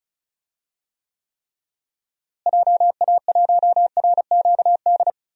Morse code, heard as a keyed tone, about 36 wpm, 700 Hz. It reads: JA1RQD
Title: DXCC-100 Call Signs - 6 Characters - Repeated 3x - 35wpm